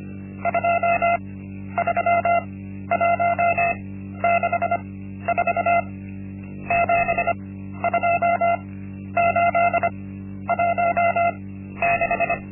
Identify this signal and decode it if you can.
Morse code, heard as a keyed tone, about 25 wpm, spaced as for 12 wpm, 690 Hz. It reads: 23164 72816
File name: Linksmakalnis-1